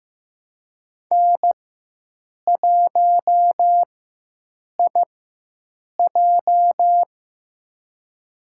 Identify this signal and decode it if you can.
Morse code, heard as a keyed tone, about 15 wpm, spaced as for 7 wpm, 700 Hz. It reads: N1IJ
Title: USA Call Signs - 4 Characters - 4x Character Spacing - 15wpm